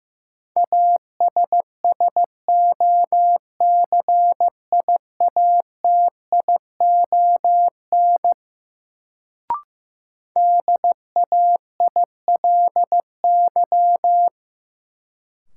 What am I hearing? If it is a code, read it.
Morse code, heard as a keyed tone, about 15 wpm, 700 Hz. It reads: ASSOCIATION DAILY
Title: Sentences from Top 1100 Words - New Word Drill Alt - 15wpm